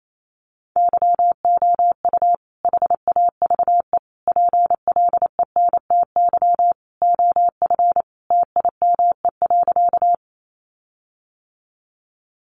Morse code, heard as a keyed tone, about 28 wpm, 700 Hz. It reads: YOU HAVE PLENTY OF TIME.
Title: Sentences from Top 1300 Words 28wpm